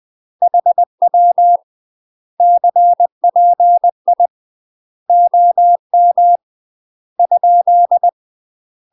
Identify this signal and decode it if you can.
Morse code, heard as a keyed tone, about 20 wpm, 700 Hz. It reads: HW CPI OM ?
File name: UK MASTER QSOs TRAINING / 20WPM / UK MASTER QSOs 0010